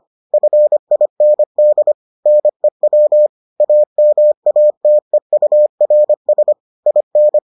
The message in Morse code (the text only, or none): FIND NEW AMATEURS IN